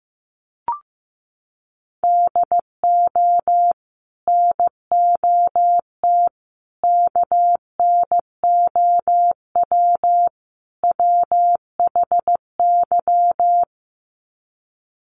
Morse code, heard as a keyed tone, about 15 wpm, 700 Hz. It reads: DO NOT KNOW WHY
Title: Sets of 4 Words 15wpm